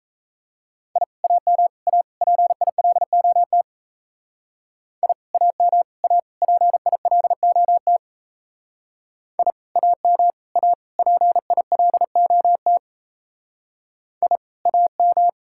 Morse code, heard as a keyed tone, about 34 wpm, 700 Hz. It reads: I AM A PILOT I AM A PILOT I AM A PILOT I AM
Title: Speed-Racing Common QSO Phrases - 28wpm